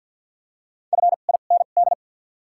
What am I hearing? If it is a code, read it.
Morse code, heard as a keyed tone, about 48 wpm, spaced as for 35 wpm, 700 Hz. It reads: FIND
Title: 2x Character Spacing - Top 100 Words - 50wpm